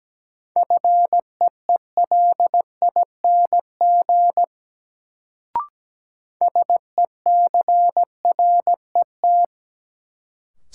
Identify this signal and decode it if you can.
Morse code, heard as a keyed tone, about 17 wpm, 700 Hz. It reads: FEELING SECRET
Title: Sentences from Top 1200 Words - New Word Drill Alt - 17wpm